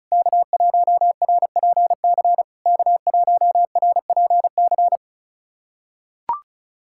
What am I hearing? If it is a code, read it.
Morse code, heard as a keyed tone, about 36 wpm, 690 Hz. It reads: K1RPC K1RPC
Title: SOTA QSO Practice 35wpm